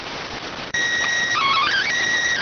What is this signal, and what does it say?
SSTV (slow-scan television), PD 290.